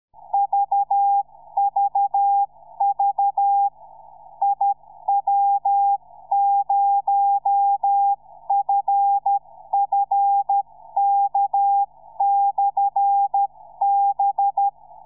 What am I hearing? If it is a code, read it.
Morse code, heard as a keyed tone, about 13 wpm, 800 Hz. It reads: VVV IW0FFK/B
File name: iw0ffk / 2007 04 22 iw0ffk beacon